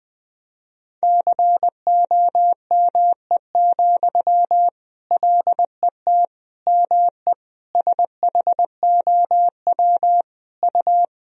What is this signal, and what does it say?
Morse code, heard as a keyed tone, about 20 wpm, 700 Hz. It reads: COME, LET ME SHOW U
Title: Sentences from Top 500 Words and Abbreviations 20wpm